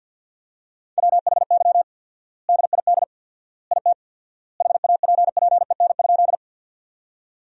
Morse code, heard as a keyed tone, about 50 wpm, 700 Hz. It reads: WHY DID IT HAPPEN?
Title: 4x Word-Spacing - Sentences from Top 400 Words - 50wpm